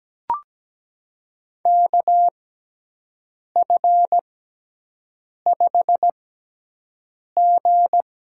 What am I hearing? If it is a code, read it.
Morse code, heard as a keyed tone, about 17 wpm, spaced as for 6 wpm, 700 Hz. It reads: KF5G
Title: USA Call Signs - 6 Characters - 6x Character Spacing - 17wpm